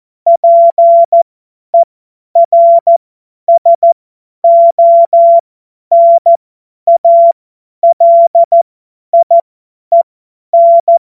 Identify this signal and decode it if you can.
Morse code, heard as a keyed tone, about 14 wpm, spaced as for 10 wpm, 680 Hz. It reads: PERSONALIEN
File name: Witz-28-Oktober-2025-50-F70